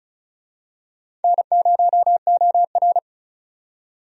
Morse code, heard as a keyed tone, about 34 wpm, 700 Hz. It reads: N0OR
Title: USA Call Signs - 4 Characters - 35wpm